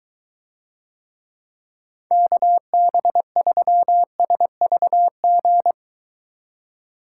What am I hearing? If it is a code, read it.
Morse code, heard as a keyed tone, about 23 wpm, 700 Hz. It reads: KB3SVG